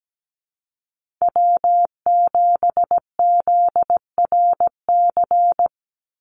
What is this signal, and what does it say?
Morse code, heard as a keyed tone, about 17 wpm, 700 Hz. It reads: W7ZRC
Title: Call Signs Repeated 3x - 17wpm